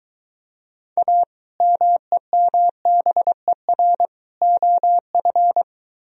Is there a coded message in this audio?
Yes: Morse code from 1.0 to 5.6 s, heard as a keyed tone, about 21 wpm, 700 Hz. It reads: A MEMBER OF